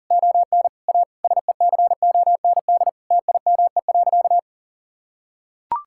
Morse code, heard as a keyed tone, about 37 wpm, 700 Hz. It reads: ON A SECOND TIME.
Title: Sentences from Top 1000 Words - Review Part 5 - 40wpm